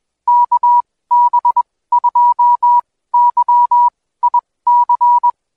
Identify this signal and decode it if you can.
Morse code, heard as a keyed tone, about 21 wpm, spaced as for 16 wpm, 970 Hz. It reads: KB2YIC
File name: Code Quick / AudioDownld2 / Seg125-DayDrill30-16WPM